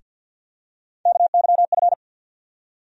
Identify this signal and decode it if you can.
Morse code, heard as a keyed tone, about 53 wpm, 700 Hz. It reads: KYF
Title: USA Call Sign Postfixes - 3 Letters 50wpm